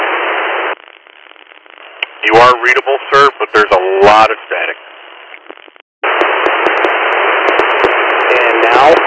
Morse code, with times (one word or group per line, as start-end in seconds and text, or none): none